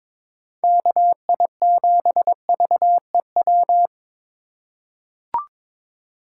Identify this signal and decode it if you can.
Morse code, heard as a keyed tone, about 21 wpm, 700 Hz. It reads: KI7VEW
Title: USA Call Signs - 4x Character Spacing - 22wpm